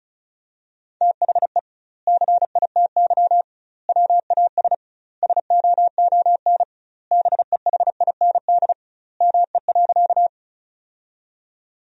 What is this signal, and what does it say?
Morse code, heard as a keyed tone, about 36 wpm, 700 Hz. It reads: THE CITY WAS SOON BEHIND ME.
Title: 2x Word-Spacing - Sentences from Top 500 Words - 35wpm